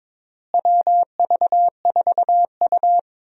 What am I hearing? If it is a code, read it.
Morse code, heard as a keyed tone, about 22 wpm, 700 Hz. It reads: WV4U
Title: USA Call Signs - 4 Characters - Repeated 3x - 22wpm